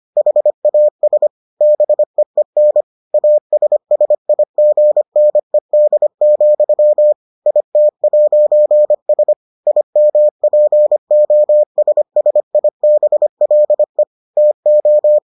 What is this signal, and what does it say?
Morse code, heard as a keyed tone, about 25 wpm, 600 Hz. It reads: HAS BEEN ASSIGNED, IT'S IMPOSSIBLE TO